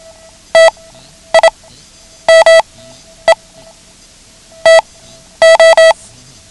Morse code, heard as a keyed tone, about 27 wpm, spaced as for 10 wpm, 650 Hz. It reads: TIME TO
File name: ham radio / CodeQuick / Code Quick 09